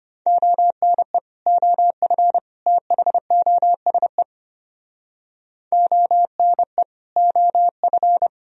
Morse code, heard as a keyed tone, about 30 wpm, 700 Hz. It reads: ONE OF THOSE ONE OF
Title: Speed-Racing - Sets of 3 Words 25wpm